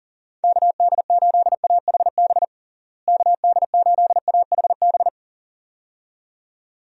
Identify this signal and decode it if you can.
Morse code, heard as a keyed tone, about 40 wpm, 700 Hz. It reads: KD8AHB KD8AHB
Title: USA Call Signs - 6 Characters - Repeated 2x - 40wpm